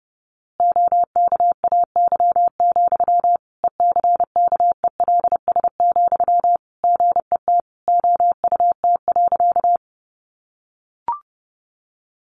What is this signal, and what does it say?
Morse code, heard as a keyed tone, about 30 wpm, 700 Hz. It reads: OKAY, ECKELS, GET OUT.
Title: A Sound Of Thunder 30wpm